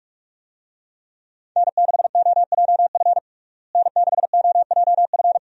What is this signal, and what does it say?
Morse code, heard as a keyed tone, about 46 wpm, 700 Hz. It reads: N6OJF N6OJF
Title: USA Call Signs - 5 Characters - Repeated 2x - 45wpm